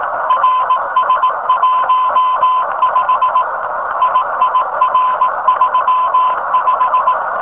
Morse code, heard as a keyed tone, about 18 wpm, 1000 Hz, with noise. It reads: RS15 IIR35
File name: rs15 telem